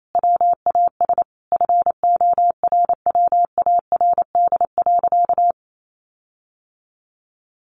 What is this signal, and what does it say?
Morse code, heard as a keyed tone, about 28 wpm, 700 Hz. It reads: WAS FORWARD.
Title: Sentences from Top 1500 Words 28wpm